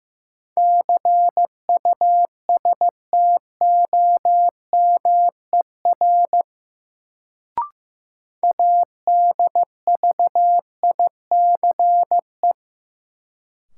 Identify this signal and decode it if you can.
Morse code, heard as a keyed tone, about 15 wpm, 700 Hz. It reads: CUSTOMER ADVICE
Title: Sentences from Top 1100 Words - New Word Drill - 15wpm